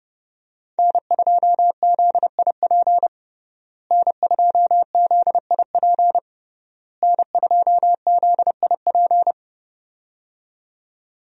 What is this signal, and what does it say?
Morse code, heard as a keyed tone, about 30 wpm, 700 Hz. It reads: N2ZIP N2ZIP N2ZIP
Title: USA Call Signs - 5 Characters - Repeated 3x - 30wpm